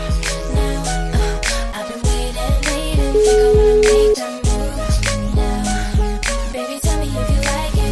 Ringback tone (Europe, 425 Hz), with noise.